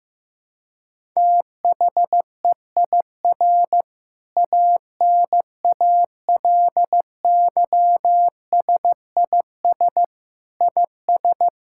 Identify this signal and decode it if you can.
Morse code, heard as a keyed tone, about 15 wpm, 700 Hz. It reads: THEIR ANALYSIS IS
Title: Sentences from Top 1000 Words - Review Part 5 - 15wpm